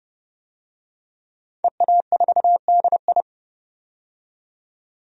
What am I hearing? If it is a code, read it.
Morse code, heard as a keyed tone, about 29 wpm, 690 Hz. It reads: EA4DI
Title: DXCC-100 Call Signs - 5 Characters - 30wpm